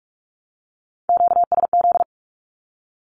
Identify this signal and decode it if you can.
Morse code, heard as a keyed tone, about 44 wpm, 700 Hz. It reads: QSZ
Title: Q-Codes Repeated 3x 45wpm